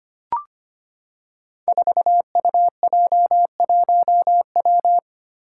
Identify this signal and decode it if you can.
Morse code, heard as a keyed tone, about 25 wpm, 700 Hz. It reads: 4UJ1W
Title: Five Challenging Letter-Number Combinations 25wpm